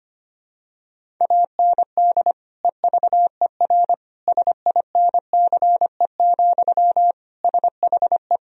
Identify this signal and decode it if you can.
Morse code, heard as a keyed tone, about 25 wpm, 700 Hz. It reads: AND EVER SINCE, SHE
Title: Sentences from Top 1000 Words - Review Part 1 - 25wpm